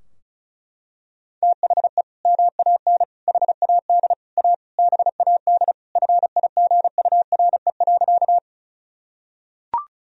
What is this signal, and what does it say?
Morse code, heard as a keyed tone, about 36 wpm, 700 Hz. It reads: THE MAN HAD A BAD FIGURE.
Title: Sentences from Top 1000 Words - Review Part 2 - 35wpm